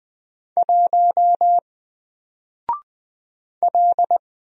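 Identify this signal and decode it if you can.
Morse code, heard as a keyed tone, about 20 wpm, 700 Hz. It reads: 1 L
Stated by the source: Single letter and number Morse drill at 20 wpm, from N4LCD.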